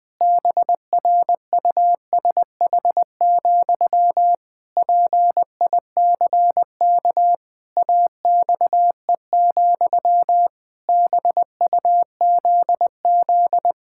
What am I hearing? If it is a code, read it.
Morse code, heard as a keyed tone, about 20 wpm, 700 Hz. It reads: BRUSH, PICK AXE, BUZZ
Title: Pangram Challenge v2 - 20wpm